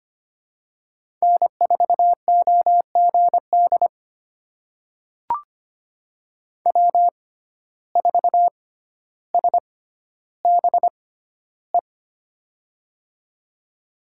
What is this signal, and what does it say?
Morse code, heard as a keyed tone, about 25 wpm, spaced as for 13 wpm, 700 Hz. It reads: N4OGD W4SBE
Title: USA Call Signs - 6x Character Spacing - 25wpm